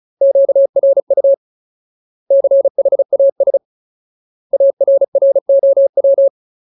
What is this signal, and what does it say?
Morse code, heard as a keyed tone, about 34 wpm, 550 Hz. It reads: QRU CHAS ARROW
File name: Ultimate Word List / 35WPM Ultimate